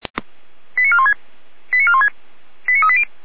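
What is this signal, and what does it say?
Five-tone selective call (ZVEI), heard as quick runs of five tones, tones of about 70 ms. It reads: 89416 89416 89380
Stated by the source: ZVEI five-tone call 89350, received by SDR.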